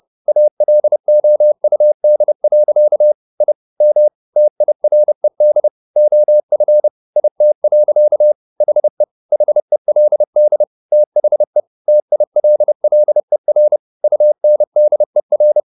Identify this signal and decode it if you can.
Morse code, heard as a keyed tone, about 30 wpm, 600 Hz. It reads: ALOUD. I M TIRED OF IT. HE HELD THE TILLER UNDER